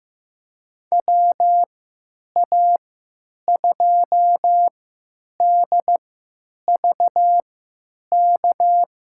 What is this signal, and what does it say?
Morse code, heard as a keyed tone, about 15 wpm, spaced as for 9 wpm, 700 Hz. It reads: WA2DVK
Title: USA Call Signs - 6 Characters - 3x Character Spacing - 15wpm